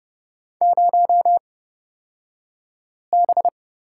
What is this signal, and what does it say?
Morse code, heard as a keyed tone, about 31 wpm, spaced as for 5 wpm, 700 Hz. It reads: 0B
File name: Lesson 37d - Letter x - Two Characters 30wpm